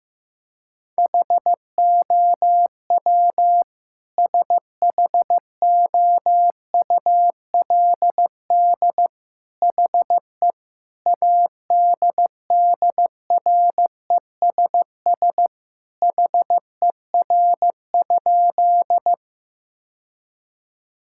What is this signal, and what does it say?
Morse code, heard as a keyed tone, about 15 wpm, 700 Hz. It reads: HOW SHOULD HE ADDRESS HER?
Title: Sentences from Top 1000 Words - Review Part 6 - 15wpm